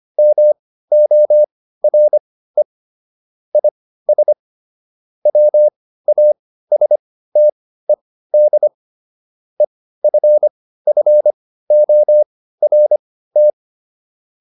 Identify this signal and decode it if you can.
Morse code, heard as a keyed tone, about 25 wpm, spaced as for 15 wpm, 600 Hz. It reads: MORE IS WASTED EFFORT